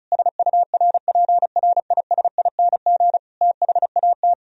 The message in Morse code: SURPRISING THAT

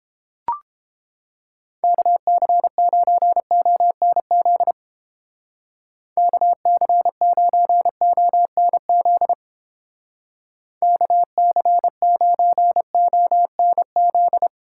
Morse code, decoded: KC9ONZ KC9ONZ KC9ONZ

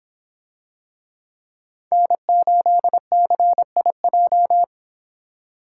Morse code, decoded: N8CIJ

